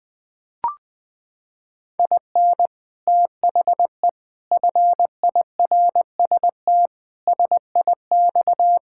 Morse code, heard as a keyed tone, about 20 wpm, 700 Hz. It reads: IN THE FIRST SIX
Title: Sets of 8 Words - 20wpm